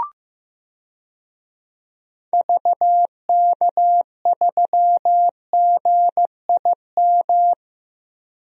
Morse code, heard as a keyed tone, about 15 wpm, 700 Hz. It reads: VK3GIM
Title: DXCC-100 Call Signs - 6 Characters - 15wpm